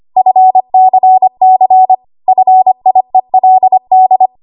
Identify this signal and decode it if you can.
Morse code, heard as a keyed tone, about 25 wpm, 750 Hz. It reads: FCC FIELD
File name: Archive / 25 WPM / 25WPM